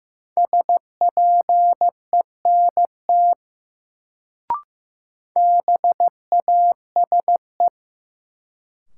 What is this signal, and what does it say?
Morse code, heard as a keyed tone, about 15 wpm, 700 Hz. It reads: SPENT BASE
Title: Sets of 5 Words - New Word Drill - 15wpm